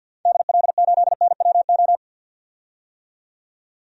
Morse code, heard as a keyed tone, about 48 wpm, 700 Hz. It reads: DL8NWO
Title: DXCC-100 Call Signs - 6 Characters - 50wpm